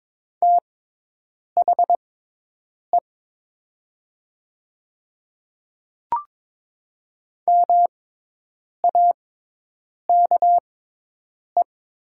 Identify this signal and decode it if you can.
Morse code, heard as a keyed tone, about 22 wpm, spaced as for 8 wpm, 700 Hz. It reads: THE MAKE
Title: 6x Character Spacing - Top 200 Words - 22wpm